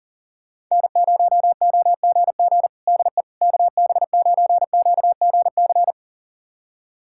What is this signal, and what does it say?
Morse code, heard as a keyed tone, about 40 wpm, 700 Hz. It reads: N0OGG DE KB9QGC
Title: USA Callsign DE Callsign 40wpm